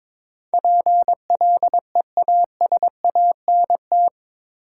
Morse code, heard as a keyed tone, about 22 wpm, 700 Hz. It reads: PLEASANT
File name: practice / 22wpm / Eight Letter Words - 22wpm